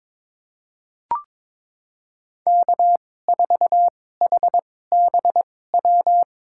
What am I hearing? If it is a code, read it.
Morse code, heard as a keyed tone, about 22 wpm, spaced as for 16 wpm, 700 Hz. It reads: K4HBW
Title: USA Call Signs - 5 Characters - 2x Character Spacing - 22wpm